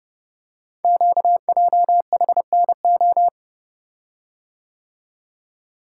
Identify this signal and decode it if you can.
Morse code, heard as a keyed tone, about 30 wpm, 700 Hz. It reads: QJHNO
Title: Code Groups - 5 Letters - 30wpm